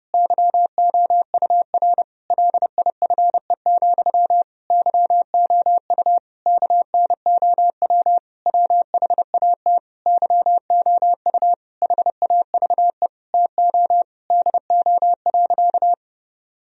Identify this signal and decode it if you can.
Morse code, heard as a keyed tone, about 30 wpm, 700 Hz. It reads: YOUR LIFE, YOU KNOW WHAT YOU HAVE TO DO.
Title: Sentences from Top 1000 Words - Review Part 3 - 30wpm